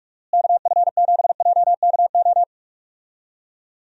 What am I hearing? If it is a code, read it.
Morse code, heard as a keyed tone, about 42 wpm, 700 Hz. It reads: KF7JKO
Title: USA Call Signs - 6 Characters - Repeated 2x - 45wpm